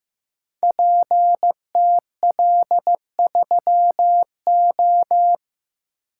Morse code, heard as a keyed tone, about 15 wpm, 700 Hz. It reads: PTL3O
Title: Code Groups - 5 Letter:Numbers - 15wpm